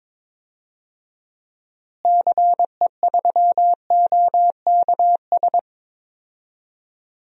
Morse code, heard as a keyed tone, about 23 wpm, 700 Hz. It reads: CE3OKS
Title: DXCC-100 Call Signs - 6 Characters - Repeated 2x - 22wpm